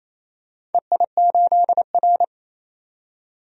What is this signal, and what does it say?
Morse code, heard as a keyed tone, about 28 wpm, 700 Hz. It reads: EI8R
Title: DXCC-100 Call Signs - 4 Characters - 28wpm